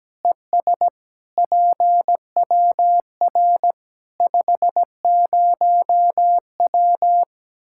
Morse code, heard as a keyed tone, about 17 wpm, 700 Hz. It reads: ES PWR 50W
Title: Practice QSO - Exchange 2 - Info - 17wpm